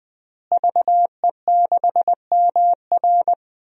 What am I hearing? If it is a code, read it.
Morse code, heard as a keyed tone, about 20 wpm, 700 Hz. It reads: VE6MR
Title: DXCC-100 Call Signs - 5 Characters - Repeated 3x - 20wpm